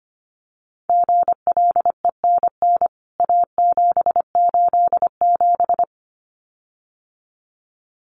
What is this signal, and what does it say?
Morse code, heard as a keyed tone, about 25 wpm, 700 Hz. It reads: GLENN A787
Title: CWT Exchange 25wpm